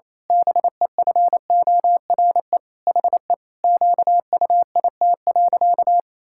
Morse code, heard as a keyed tone, about 28 wpm, 700 Hz. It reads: BEFORE HE QUIT.
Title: Pangram Challenge v2 - 28wpm